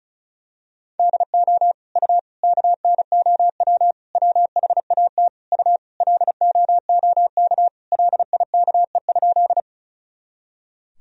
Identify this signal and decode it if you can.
Morse code, heard as a keyed tone, about 36 wpm, 700 Hz. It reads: DO U KNOW WHAT U LOOK LIKE?
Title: Sentences from Top 100 Words and Abbreviations 35wpm